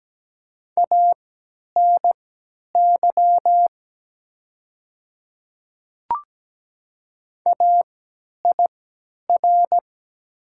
Morse code, heard as a keyed tone, about 17 wpm, spaced as for 10 wpm, 700 Hz. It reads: ANY AIR
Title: 3x Character Spacing - Top 200 Words - 17wpm